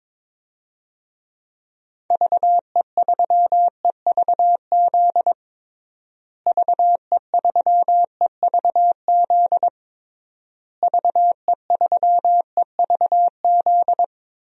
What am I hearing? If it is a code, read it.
Morse code, heard as a keyed tone, about 22 wpm, 700 Hz. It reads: VE3EVZ VE3EVZ VE3EVZ
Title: DXCC-100 Call Signs - 6 Characters - Repeated 3x - 22wpm